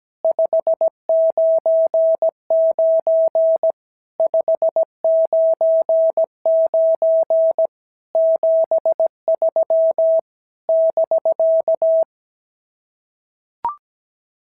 Morse code, heard as a keyed tone, about 17 wpm, 650 Hz. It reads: 599 599 73 <BK>
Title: SOTA QSO Practice 17wpm